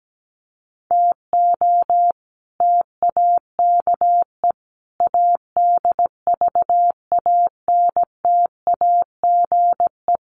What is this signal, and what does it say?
Morse code, heard as a keyed tone, about 17 wpm, 700 Hz. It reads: TO TAKE ADVANTAGE
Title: Sets of 5 Words 17wpm